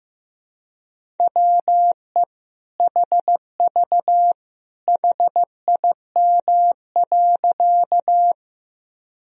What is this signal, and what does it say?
Morse code, heard as a keyed tone, about 15 wpm, 700 Hz. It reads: WE HV HIM.
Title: Sentences from Top 100 Words and Abbreviations 15wpm